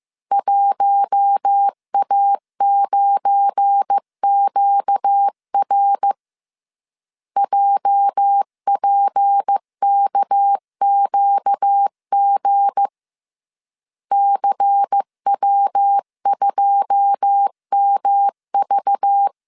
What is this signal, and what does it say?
Morse code, heard as a keyed tone, about 15 wpm, 800 Hz. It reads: 1A9QR JPKQG CW2MV
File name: CW / DET / 1 9 1500C S A 14wpm 3222